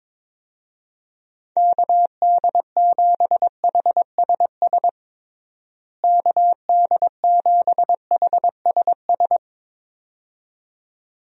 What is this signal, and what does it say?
Morse code, heard as a keyed tone, about 22 wpm, 700 Hz. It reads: KD7HSS KD7HSS
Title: USA Call Signs - 6 Characters - Repeated 2x - 22wpm